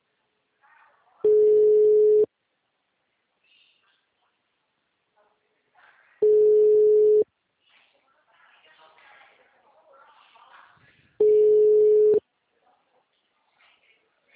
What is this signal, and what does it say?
Ringback tone (Europe, 425 Hz), with speech.